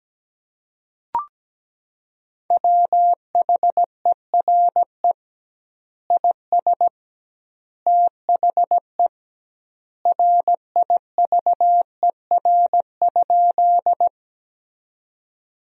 Morse code, heard as a keyed tone, about 17 wpm, 700 Hz. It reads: WHERE IS THE RIVER?
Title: 2x Word-Spacing - Sentences from Top 300 Words - 17wpm